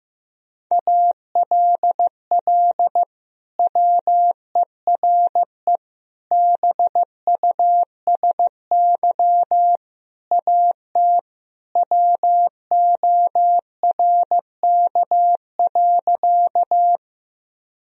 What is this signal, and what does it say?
Morse code, heard as a keyed tone, about 15 wpm, 700 Hz. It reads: ALL WERE BUSY AT WORK.
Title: Sentences from Top 1000 Words - Review Part 3 - 15wpm